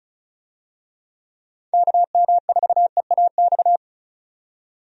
Morse code, heard as a keyed tone, about 36 wpm, 700 Hz. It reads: KM4EAX
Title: USA Call Signs - 6 Characters - 35wpm